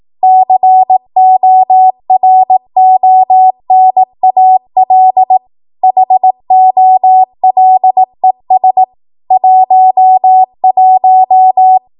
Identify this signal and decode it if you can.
Morse code, heard as a keyed tone, about 18 wpm, 750 Hz. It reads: CORONAL HOLES 11
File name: Morse / 18WPM